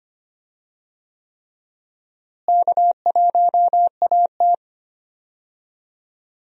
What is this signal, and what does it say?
Morse code, heard as a keyed tone, about 25 wpm, 700 Hz. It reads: K1AT
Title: USA Call Signs - 4 Characters - Repeated 3x - 25wpm